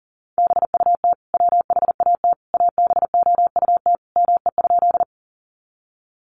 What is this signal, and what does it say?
Morse code, heard as a keyed tone, about 40 wpm, 700 Hz. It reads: BUT WHAT ABOUT ME?
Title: Sentences from Top 200 Words 40wpm